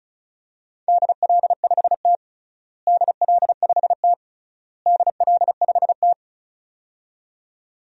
Morse code, heard as a keyed tone, about 36 wpm, 700 Hz. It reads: DL5T DL5T DL5T
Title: DXCC-100 Call Signs - 4 Characters - Repeated 3x - 35wpm